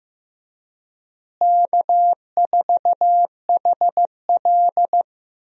Morse code, heard as a keyed tone, about 15 wpm, 700 Hz. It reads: K4HL
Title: USA Call Signs - 4 Characters - Repeated 3x - 15wpm